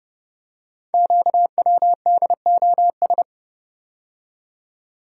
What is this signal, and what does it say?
Morse code, heard as a keyed tone, about 31 wpm, 700 Hz. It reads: QWDOS
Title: Code Groups - 5 Letters - 30wpm